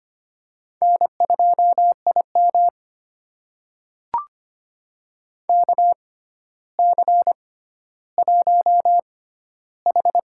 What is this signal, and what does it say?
Morse code, heard as a keyed tone, about 25 wpm, spaced as for 13 wpm, 700 Hz. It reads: N2IM KC1H
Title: USA Call Signs - 4 Characters - 6x Character Spacing - 25wpm